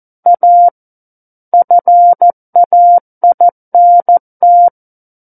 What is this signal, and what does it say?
Morse code, heard as a keyed tone, about 14 wpm, 700 Hz. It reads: A FAINT
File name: Book One / 05-Chapter 5 - 14 WPM